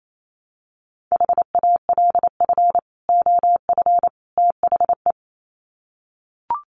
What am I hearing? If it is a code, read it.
Morse code, heard as a keyed tone, about 28 wpm, 700 Hz. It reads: HALF OF THE
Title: Sets of 3 Words 28wpm